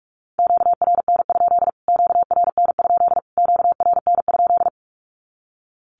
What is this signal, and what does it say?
Morse code, heard as a keyed tone, about 44 wpm, 700 Hz. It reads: QRN? QRN? QRN?
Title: Q-Codes Repeated 3x 45wpm